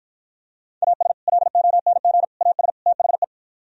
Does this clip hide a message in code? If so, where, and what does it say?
Yes: Morse code from 0.8 to 3.2 s, heard as a keyed tone, about 52 wpm, 700 Hz. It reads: AS LONG AS THE